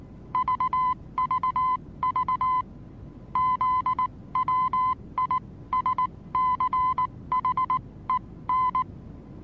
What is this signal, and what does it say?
Morse code, heard as a keyed tone, about 19 wpm, spaced as for 16 wpm, 1020 Hz. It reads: VVV ZWISCHEN
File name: repos / Morsekurs / DJ4UF Morsen26